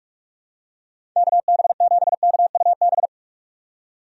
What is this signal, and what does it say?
Morse code, heard as a keyed tone, about 46 wpm, 700 Hz. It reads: KB7KUB
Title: USA Call Signs - 6 Characters - 45wpm